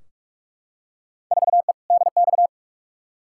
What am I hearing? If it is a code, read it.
Morse code, heard as a keyed tone, about 46 wpm, 700 Hz. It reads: 4E DX
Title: Field Day Exchanges Encore - 45wpm